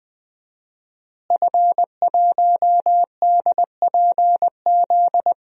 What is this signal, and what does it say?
Morse code, heard as a keyed tone, about 20 wpm, 700 Hz. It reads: F1DPZ
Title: DXCC-100 Call Signs - 5 Characters - Repeated 3x - 20wpm